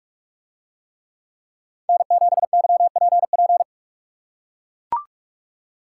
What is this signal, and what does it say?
Morse code, heard as a keyed tone, about 44 wpm, 700 Hz. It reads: N7YPP